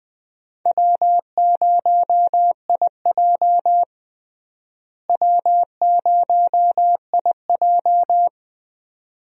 Morse code, heard as a keyed tone, about 20 wpm, 700 Hz. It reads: W0IJ W0IJ
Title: USA Call Signs - 4 Characters - Repeated 2x - 20wpm